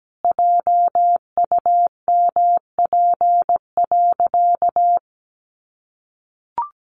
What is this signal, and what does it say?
Morse code, heard as a keyed tone, about 17 wpm, 700 Hz. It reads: JUMP.
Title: Sentences from Top 1300 Words 17wpm